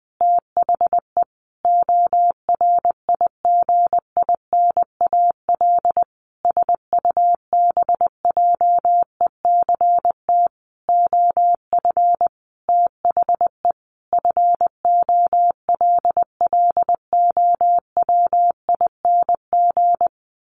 Morse code, heard as a keyed tone, about 20 wpm, 700 Hz. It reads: THE ORIGINAL SUBJECT OF THE FOLLOWING
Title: Sentences from Top 900 Words 20wpm